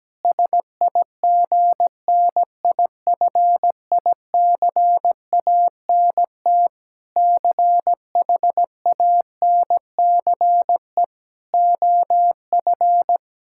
Morse code, heard as a keyed tone, about 17 wpm, 700 Hz. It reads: SIGNIFICANT CHANCE OF
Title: Sentences from Top 1000 Words - Review Part 6 - 17wpm